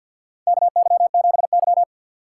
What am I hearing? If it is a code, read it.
Morse code, heard as a keyed tone, about 46 wpm, 700 Hz. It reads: KY7Y
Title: USA Call Signs - 4 Characters - 3x Character Spacing - 50wpm